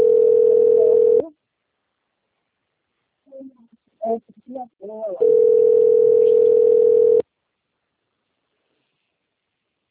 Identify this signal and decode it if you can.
Ringback tone (North America), 440 + 480 Hz, with speech.
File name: rg-1600-unknown- - - .14419